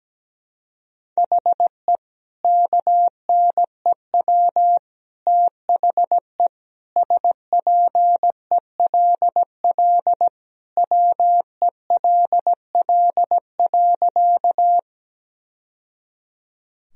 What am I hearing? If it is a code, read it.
Morse code, heard as a keyed tone, about 17 wpm, 700 Hz. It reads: HE KNEW THE SPELL WELL.
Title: Sentences from Top 1000 Words - Review Part 1 - 17wpm